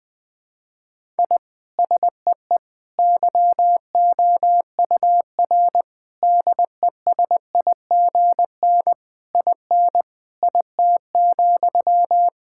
Morse code, heard as a keyed tone, about 20 wpm, 700 Hz. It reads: I SEE YOUR DESIGN IN IT,
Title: Sentences from Top 1000 Words - Review Part 5 - 20wpm